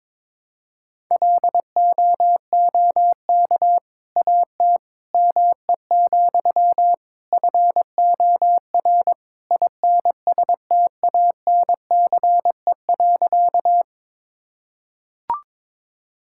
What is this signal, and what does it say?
Morse code, heard as a keyed tone, about 22 wpm, 700 Hz. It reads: LOOK AT ME, FOR INSTANCE.
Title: Sentences from Top 1000 Words - Review Part 5 - 22wpm